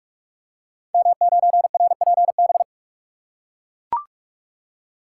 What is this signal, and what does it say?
Morse code, heard as a keyed tone, about 44 wpm, 690 Hz. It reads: M9RPB
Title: Code Groups - 5 Letter:Numbers - 45wpm